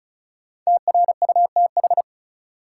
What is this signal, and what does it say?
Morse code, heard as a keyed tone, about 34 wpm, 700 Hz. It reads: TRUTH
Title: Sentences from Top 800 Words - New Word Drill - 35wpm